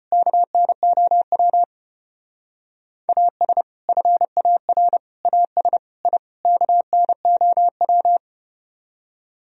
Morse code, heard as a keyed tone, about 33 wpm, 700 Hz. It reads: KNOW AS FAR AS I KNOW